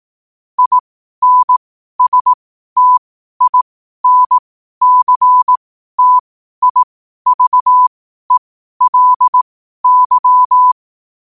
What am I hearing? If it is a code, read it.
Morse code, heard as a keyed tone, about 18 wpm, spaced as for 12 wpm, 1000 Hz. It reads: INSTINCTIVELY
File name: qotdfiles / 12wpm